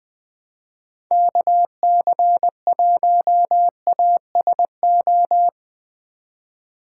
Morse code, heard as a keyed tone, about 20 wpm, 700 Hz. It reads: KC1ASO
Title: USA Call Signs - 6 Characters - 3x Character Spacing - 20wpm